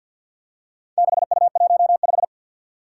Morse code, heard as a keyed tone, about 53 wpm, 700 Hz. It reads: 6U15
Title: Four Challenging Letter-Number Combinations 50wpm